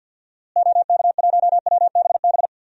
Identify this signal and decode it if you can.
Morse code, heard as a keyed tone, about 52 wpm, 700 Hz. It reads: OK1WBB
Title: DXCC-100 Call Signs - 6 Characters - Repeated 2x - 50wpm